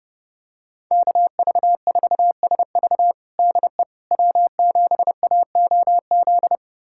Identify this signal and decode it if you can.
Morse code, heard as a keyed tone, about 32 wpm, 700 Hz. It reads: KV4SV DE W7AOZ
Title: USA Callsign DE Callsign - 5 Characters - 30wpm